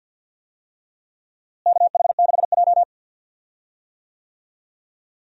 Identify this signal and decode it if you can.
Morse code, heard as a keyed tone, about 52 wpm, 700 Hz. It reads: KH6J
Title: USA Call Signs - 4 Characters - Repeated 3x - 50wpm